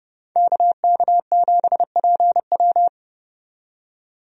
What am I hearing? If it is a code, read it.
Morse code, heard as a keyed tone, about 30 wpm, 700 Hz. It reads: KK7PW
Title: USA Call Signs - 5 Characters - 30wpm